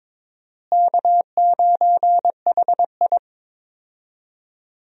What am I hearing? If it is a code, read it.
Morse code, heard as a keyed tone, about 22 wpm, 700 Hz. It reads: K9HI
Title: USA Call Signs - 4 Characters - Repeated 2x - 22wpm